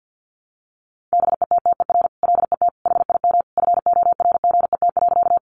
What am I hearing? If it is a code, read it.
Morse code, heard as a keyed tone, about 50 wpm, 700 Hz. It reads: BETTER LET HIM FORGET.
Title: Sentences from Top 1200 Words 50wpm